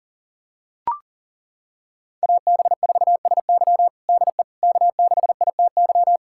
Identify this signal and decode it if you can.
Morse code, heard as a keyed tone, about 40 wpm, 700 Hz. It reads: AB4SY DE K6ITY
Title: USA Callsign DE Callsign - 5 Characters - 40wpm